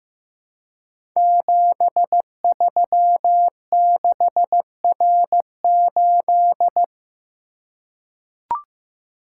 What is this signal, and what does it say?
Morse code, heard as a keyed tone, about 15 wpm, 700 Hz. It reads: 736R8
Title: Code Groups - 5 Letter:Numbers - 15wpm